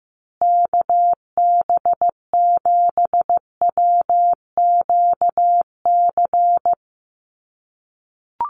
Morse code, heard as a keyed tone, about 15 wpm, 700 Hz. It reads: KB7WQC
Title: USA Call Signs Repeated 2x 15wpm